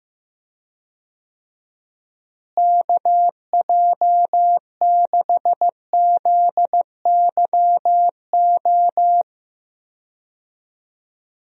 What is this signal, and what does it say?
Morse code, heard as a keyed tone, about 15 wpm, 700 Hz. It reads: KJ6ZYO
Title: USA Call Signs - 6 Characters - Repeated 3x - 15wpm